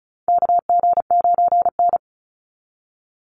Morse code, heard as a keyed tone, about 34 wpm, 700 Hz. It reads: KG9N
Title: Call Signs - 35wpm